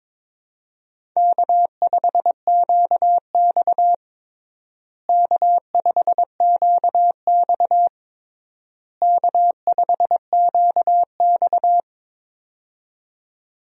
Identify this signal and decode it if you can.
Morse code, heard as a keyed tone, about 21 wpm, 700 Hz. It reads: K5QX K5QX K5QX
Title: USA Call Signs - 4 Characters - Repeated 3x - 22wpm